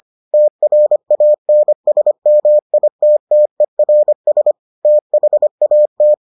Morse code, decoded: TRANSMITTERS THAT

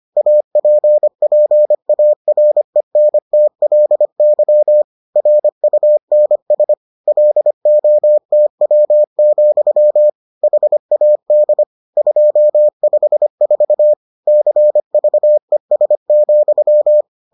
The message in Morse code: APPARENTLY RUNS LOTW, HAD 254 CVES,

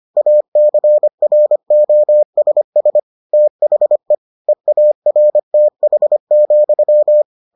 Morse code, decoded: ACROSS THE EARTH,

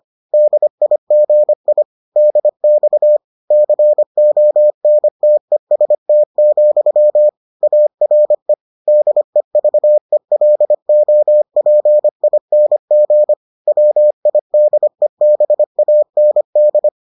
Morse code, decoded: DIGI DX CONTEST, ARE DEVELOPING WIDEBAND

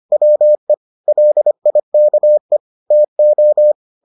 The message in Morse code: WE LIKE TO

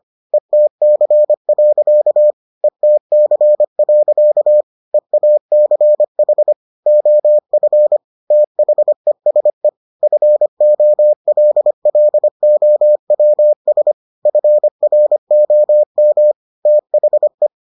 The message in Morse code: ETC. ETC. EACH OF THESE FOLLOWS FROM THE

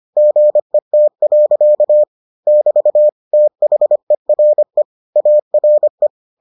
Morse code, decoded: GET. <BT> THERE ARE

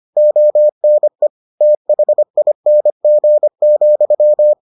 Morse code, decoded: ONE THING,

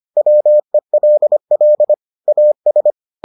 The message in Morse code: WELL AS

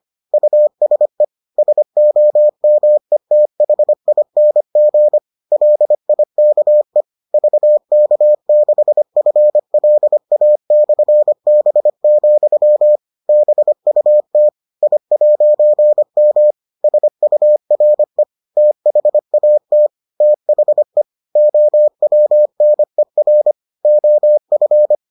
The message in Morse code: USE SOMETHING LIKE VK6FLA/B, BUT I'M SURE THAT THE OWNER OF

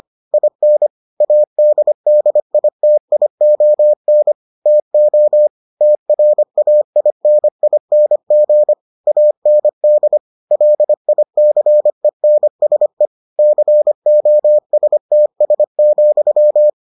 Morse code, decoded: IN ADDITION TO TRAINING AND LICENSE COSTS,

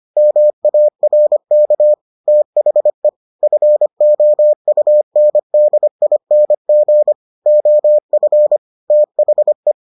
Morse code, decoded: MARK THE FOUNDING OF THE

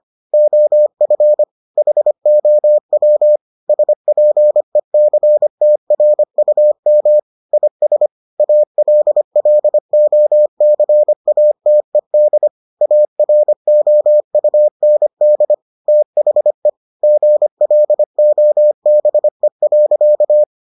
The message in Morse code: OF HOW SPECTRUM IS ALLOCATED AROUND THE GLOBE.

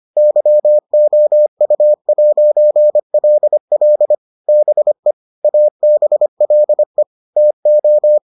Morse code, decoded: YOU'LL BE ABLE TO

